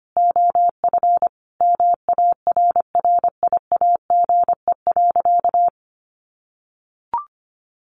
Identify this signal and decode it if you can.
Morse code, heard as a keyed tone, about 25 wpm, 700 Hz. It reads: OF MARRIAGE.
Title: Sentences from Top 800 Words 25wpm